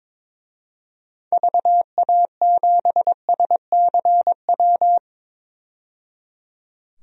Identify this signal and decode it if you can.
Morse code, heard as a keyed tone, about 22 wpm, 700 Hz. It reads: VA7SCW
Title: DXCC-100 Call Signs - 6 Characters - 22wpm